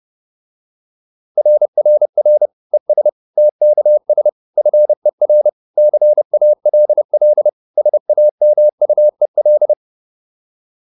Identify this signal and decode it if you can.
Morse code, heard as a keyed tone, about 30 wpm, 600 Hz. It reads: RRR ES TKS FER CALL SAMUEL